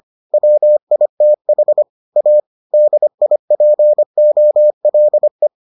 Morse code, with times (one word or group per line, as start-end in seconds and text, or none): 0.3-1.8 WITH
2.2-2.4 A
2.7-5.5 DIPOLE